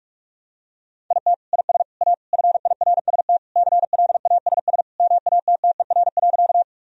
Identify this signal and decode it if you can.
Morse code, heard as a keyed tone, about 46 wpm, 700 Hz. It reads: IT IS A FIRST CLASS MATTER.
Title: Sentences from Top 1000 Words - Review Part 2 - 45wpm